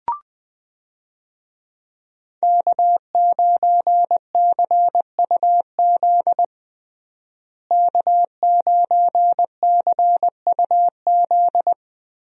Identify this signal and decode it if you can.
Morse code, heard as a keyed tone, about 21 wpm, 700 Hz. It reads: K9CUZ K9CUZ
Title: USA Call Signs - 5 Characters - Repeated 2x - 20wpm